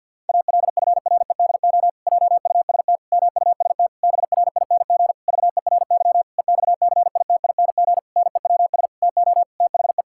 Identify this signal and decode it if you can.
Morse code, heard as a keyed tone, about 52 wpm, 700 Hz. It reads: ALFREDO JUST MUST BRING VERY EXCITING NEWS TO THE
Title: Pangram Challenge v2 - 50wpm